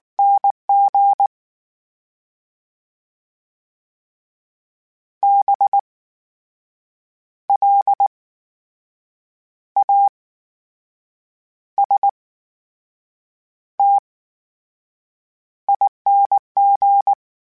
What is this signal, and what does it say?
Morse code, heard as a keyed tone, about 19 wpm, spaced as for 7 wpm, 800 Hz. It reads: NG BLASTING
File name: Special INGs / Special INGs-mp3s / VerySlowToFaster Ing @18